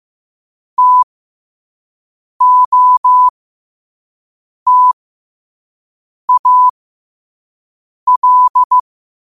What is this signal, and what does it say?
Morse code, heard as a keyed tone, about 15 wpm, spaced as for 5 wpm, 1000 Hz. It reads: TOTAL